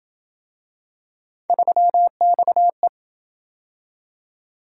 Morse code, heard as a keyed tone, about 26 wpm, 700 Hz. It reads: 3XE